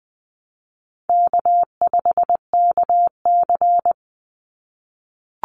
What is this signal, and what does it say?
Morse code, heard as a keyed tone, about 20 wpm, 700 Hz. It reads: K5KC
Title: Easy Call Signs - 20wpm